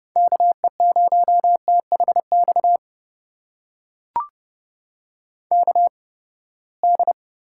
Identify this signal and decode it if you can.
Morse code, heard as a keyed tone, about 30 wpm, spaced as for 21 wpm, 700 Hz. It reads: KE0THX KD
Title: USA Call Signs - 6 Characters - 8x Character Spacing - 30wpm